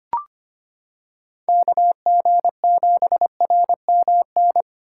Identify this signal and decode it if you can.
Morse code, heard as a keyed tone, about 25 wpm, 700 Hz. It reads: KG7RMN
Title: USA Callsign DE Callsign - 6 Characters - 25wpm